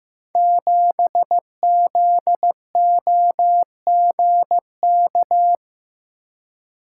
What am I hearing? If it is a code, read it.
Morse code, heard as a keyed tone, about 15 wpm, 700 Hz. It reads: 7ZOGK
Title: Code Groups - 5 Letter:Numbers - 15wpm